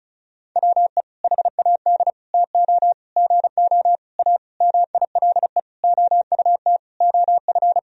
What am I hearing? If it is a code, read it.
Morse code, heard as a keyed tone, about 36 wpm, 700 Hz. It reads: WE HAD TO GO A MILE OUT OF
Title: Sentences from Top 1000 Words - Review Part 1 - 35wpm